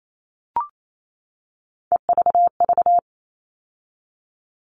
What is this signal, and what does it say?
Morse code, heard as a keyed tone, about 28 wpm, 700 Hz. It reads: EVV
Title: All Cut Numbers - 3-Digit 28wpm